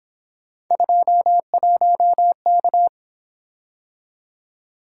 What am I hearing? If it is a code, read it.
Morse code, heard as a keyed tone, about 26 wpm, 700 Hz. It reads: 21K
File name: Speed-Racing - Three Letter-Number Combinations - 22wpm